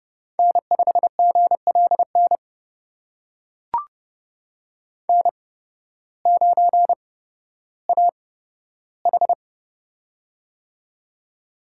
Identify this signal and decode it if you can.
Morse code, heard as a keyed tone, about 30 wpm, spaced as for 14 wpm, 700 Hz. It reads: N5GLN N9AH
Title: USA Call Signs - 8x Character Spacing - 30wpm